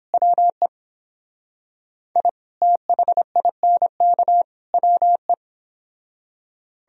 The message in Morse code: WE I THINK WE

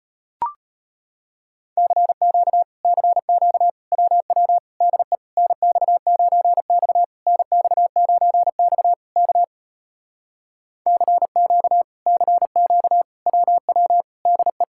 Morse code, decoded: CQ CQ WW DE NX9X NX9X K CQ CQ WW DE

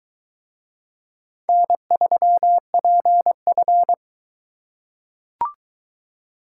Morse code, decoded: N3PF

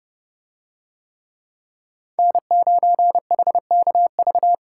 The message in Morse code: N9HKV

